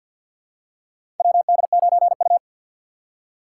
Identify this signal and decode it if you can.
Morse code, heard as a keyed tone, about 53 wpm, 700 Hz. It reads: WD9U
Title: USA Call Signs - 4 Characters - 2x Character Spacing - 50wpm